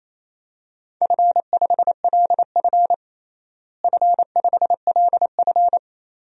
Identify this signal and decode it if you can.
Morse code, heard as a keyed tone, about 27 wpm, 700 Hz. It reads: F5LF F5LF
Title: DXCC-100 Call Signs - 4 Characters - Repeated 3x - 28wpm